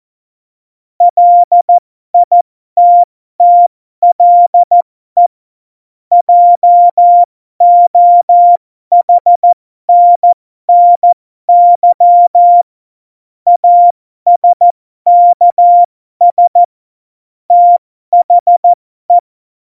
Morse code, heard as a keyed tone, about 14 wpm, spaced as for 12 wpm, 700 Hz. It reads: LITTLE JOHNNY ASKS THE
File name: Morsekurs / Witze